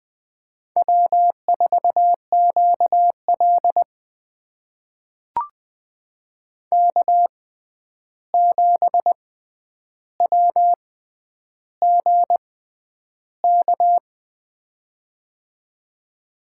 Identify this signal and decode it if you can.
Morse code, heard as a keyed tone, about 20 wpm, spaced as for 10 wpm, 700 Hz. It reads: W4QL K7WGK
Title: USA Call Signs - 6x Character Spacing - 20wpm